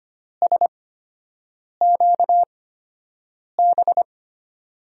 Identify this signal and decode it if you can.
Morse code, heard as a keyed tone, about 25 wpm, spaced as for 7 wpm, 700 Hz. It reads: SQB
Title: USA Call Signs - 8x Character Spacing - 25wpm